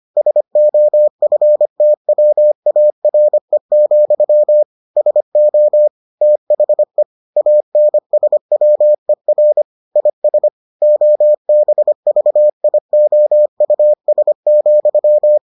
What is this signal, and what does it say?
Morse code, heard as a keyed tone, about 25 wpm, 600 Hz. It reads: SOFTWARE, SO THE ANSWER IS OBVIOUS,